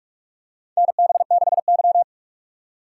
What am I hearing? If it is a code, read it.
Morse code, heard as a keyed tone, about 46 wpm, 700 Hz. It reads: NB6Y
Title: USA Call Signs - 4 Characters - 4x Character Spacing - 45wpm